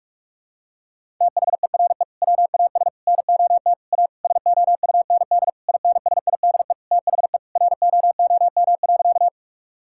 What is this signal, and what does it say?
Morse code, heard as a keyed tone, about 46 wpm, 700 Hz. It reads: THERE WAS NOT A SOUND INSIDE THE ROOM.
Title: Sentences from Top 1000 Words - Review Part 6 - 45wpm